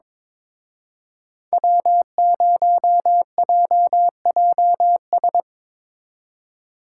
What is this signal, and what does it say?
Morse code, heard as a keyed tone, about 25 wpm, 700 Hz. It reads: W0JJS
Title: Speed Racing - USA 1x3 Call Signs 22wpm